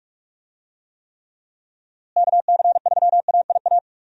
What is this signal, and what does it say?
Morse code, heard as a keyed tone, about 41 wpm, 700 Hz. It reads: KC3AIA
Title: USA Call Signs - 6 Characters - Repeated 3x - 45wpm